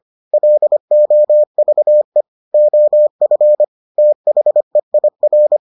Morse code, heard as a keyed tone, about 25 wpm, 600 Hz. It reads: LOVE OF THEIR